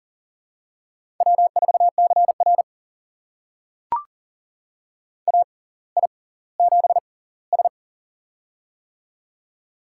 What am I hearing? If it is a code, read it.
Morse code, heard as a keyed tone, about 39 wpm, spaced as for 20 wpm, 700 Hz. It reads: W4CR AI7S
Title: USA Call Signs - 4 Characters - 6x Character Spacing - 40wpm